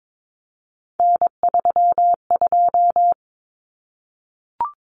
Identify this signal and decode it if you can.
Morse code, heard as a keyed tone, about 22 wpm, 700 Hz. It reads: N32
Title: Common Cut Numbers - 3-Digit 22wpm